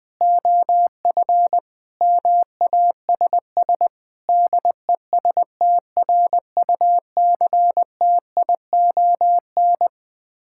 Morse code, heard as a keyed tone, about 20 wpm, 690 Hz. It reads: OF MASS DESTRUCTION
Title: Build A Phrase - 4-Word Phrases - 20wpm